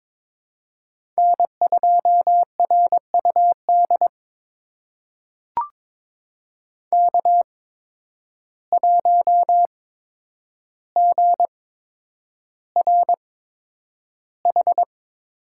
Morse code, heard as a keyed tone, about 22 wpm, spaced as for 9 wpm, 700 Hz. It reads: N2RUD K1GRH
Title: USA Call Signs - 8x Character Spacing - 22wpm